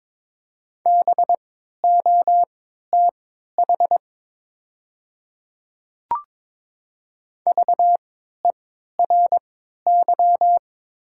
Morse code, heard as a keyed tone, about 22 wpm, spaced as for 13 wpm, 700 Hz. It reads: BOTH VERY
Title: 3x Character Spacing - Top 300 Words - 22wpm